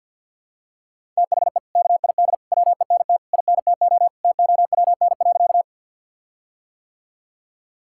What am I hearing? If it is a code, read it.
Morse code, heard as a keyed tone, about 50 wpm, 700 Hz. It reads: THE KID WENT INTO TOWN.
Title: Sentences from Top 1600 Words 50wpm